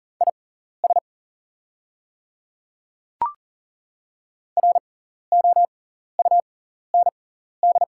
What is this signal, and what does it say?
Morse code, heard as a keyed tone, about 42 wpm, spaced as for 14 wpm, 690 Hz. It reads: IS ROUND
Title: 6x Character Spacing - Top 200 Words - 40wpm